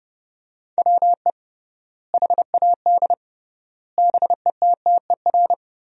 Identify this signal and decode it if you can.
Morse code, heard as a keyed tone, about 32 wpm, 700 Hz. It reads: WE HAD BETTER
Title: 3x Word-Spacing - Sentences from Top 400 Words - 30wpm